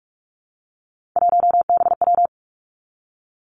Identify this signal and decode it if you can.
Morse code, heard as a keyed tone, about 46 wpm, 700 Hz. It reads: 1BW
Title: Three Challenging Letter-Number Combinations 45wpm